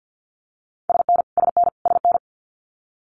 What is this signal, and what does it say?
Morse code, heard as a keyed tone, about 50 wpm, 700 Hz. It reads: SN SN SN
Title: Abbreviations Repeated 3x - 50wpm